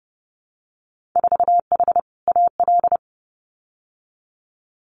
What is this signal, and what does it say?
Morse code, heard as a keyed tone, about 30 wpm, 700 Hz. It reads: 4H AL
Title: Winter Field Day Exchanges Encore 30wpm